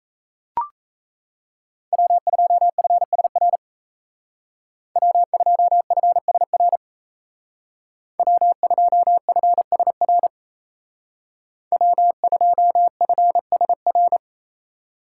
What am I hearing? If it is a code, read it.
Morse code, heard as a keyed tone, about 35 wpm, 700 Hz. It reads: W2FSR W2FSR W2FSR W2FSR
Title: Speed Racing - USA Call Signs 28wpm